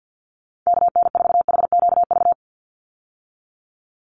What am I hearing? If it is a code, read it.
Morse code, heard as a keyed tone, about 50 wpm, 700 Hz. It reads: KN4HQV
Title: USA Call Signs 50wpm